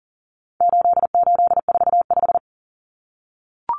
Morse code, heard as a keyed tone, about 40 wpm, 700 Hz. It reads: 8845
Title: Numbers - 4-Digit 40wpm